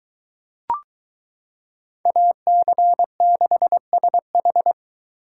Morse code, heard as a keyed tone, about 23 wpm, 700 Hz. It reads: AC6SH